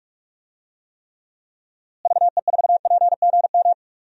Morse code, heard as a keyed tone, about 46 wpm, 700 Hz. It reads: VE4PGM
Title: DXCC-100 Call Signs - 6 Characters - Repeated 2x - 45wpm